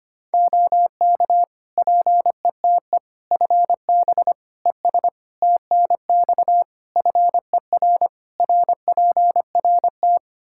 Morse code, heard as a keyed tone, about 25 wpm, 700 Hz. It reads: OK PETE FB ES TNX FER RPRT
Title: Practice QSO - Exchange 2 - Info - 25wpm